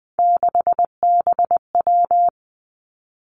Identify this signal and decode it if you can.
Morse code, heard as a keyed tone, about 19 wpm, 700 Hz. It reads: 6BW
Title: Three Challenging Letter-Number Combinations 20wpm